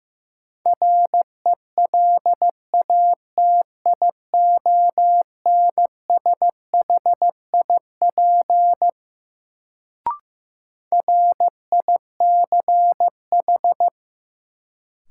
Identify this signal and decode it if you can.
Morse code, heard as a keyed tone, about 15 wpm, 700 Hz. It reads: RELATIONSHIP RICH
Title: Sentences from Top 1100 Words - New Word Drill Alt - 15wpm